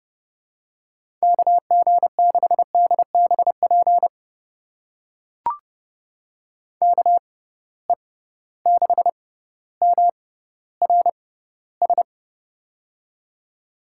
Morse code, heard as a keyed tone, about 30 wpm, spaced as for 15 wpm, 700 Hz. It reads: KG6DBP KE6MRS
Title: USA Call Signs - 6 Characters - 6x Character Spacing - 30wpm